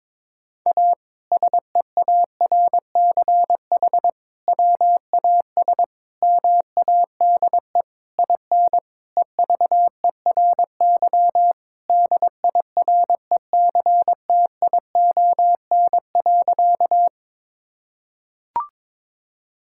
Morse code, heard as a keyed tone, about 22 wpm, 700 Hz. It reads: A SEARCH WAS MADE IN EVERY DIRECTION.
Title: Sentences from Top 1000 Words - Review Part 5 - 22wpm